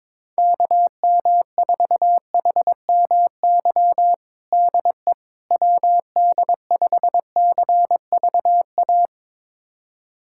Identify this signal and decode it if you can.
Morse code, heard as a keyed tone, about 22 wpm, 700 Hz. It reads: KM4HMY DE WD5CVA
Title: USA Callsign DE Callsign - 6 Characters - 22wpm